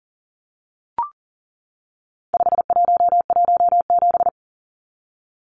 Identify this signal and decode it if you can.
Morse code, heard as a keyed tone, about 40 wpm, 700 Hz. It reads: 5117
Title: Numbers - 4-Digit 40wpm